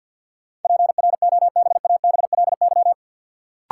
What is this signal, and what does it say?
Morse code, heard as a keyed tone, about 53 wpm, 700 Hz. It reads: PROBABLY
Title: Sentences from Top 700 Words - New Word Drill - 50wpm